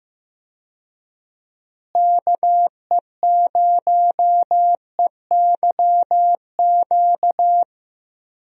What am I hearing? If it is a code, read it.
Morse code, heard as a keyed tone, about 15 wpm, 700 Hz. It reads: KE0EYQ
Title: USA Call Signs - 6 Characters - Repeated 2x - 15wpm